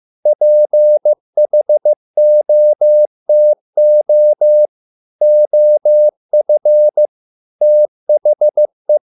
Morse code, heard as a keyed tone, about 15 wpm, 600 Hz. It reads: PHOTO OF THE